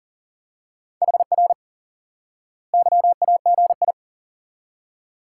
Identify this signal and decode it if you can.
Morse code, heard as a keyed tone, about 40 wpm, 700 Hz. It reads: HR YAGI
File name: IQE Course 3 - Lesson 89 - YAGI - Two Elements - 40wpm